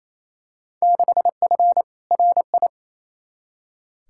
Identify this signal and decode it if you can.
Morse code, heard as a keyed tone, about 28 wpm, 700 Hz. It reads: 6F RI
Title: Field Day Exchanges Encore - 28wpm